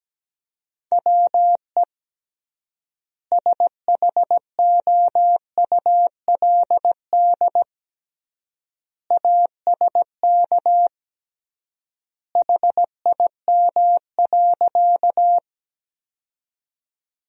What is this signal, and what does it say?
Morse code, heard as a keyed tone, about 17 wpm, 700 Hz. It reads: WE SHOULD ASK HIM.
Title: 3x Word-Spacing - Sentences from Top 300 Words - 17wpm